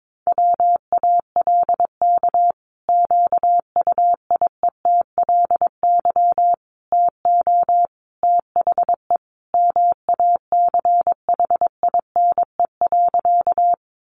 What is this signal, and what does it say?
Morse code, heard as a keyed tone, about 22 wpm, 700 Hz. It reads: WALK QUIETLY TO THE MACHINE.
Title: A Sound Of Thunder 22wpm